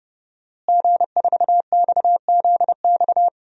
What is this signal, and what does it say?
Morse code, heard as a keyed tone, about 31 wpm, 700 Hz. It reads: G4XZX
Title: DXCC-100 Call Signs - 5 Characters - Repeated 3x - 30wpm